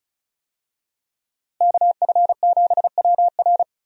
Morse code, heard as a keyed tone, about 36 wpm, 700 Hz. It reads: KF7WR
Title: USA Call Signs - 5 Characters - Repeated 3x - 35wpm